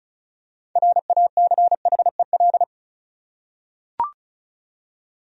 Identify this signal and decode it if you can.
Morse code, heard as a keyed tone, about 34 wpm, 700 Hz. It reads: RACHEL
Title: USA Common Names - 35wpm